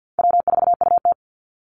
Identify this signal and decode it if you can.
Morse code, heard as a keyed tone, about 50 wpm, 700 Hz. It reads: W4UT
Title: Easy Call Signs - 50wpm